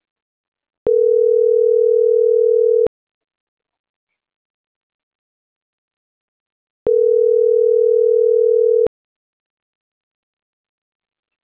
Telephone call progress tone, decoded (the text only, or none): ringback tone